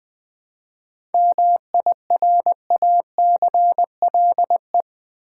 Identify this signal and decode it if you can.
Morse code, heard as a keyed tone, about 20 wpm, 700 Hz. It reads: MIRACLE
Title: 7 Letter Words - First 3 Letters Given - 20wpm